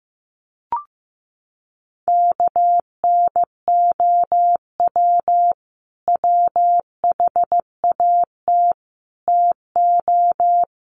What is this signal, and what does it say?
Morse code, heard as a keyed tone, about 15 wpm, 700 Hz. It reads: KNOW WHAT TO
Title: Sets of 5 Words 15wpm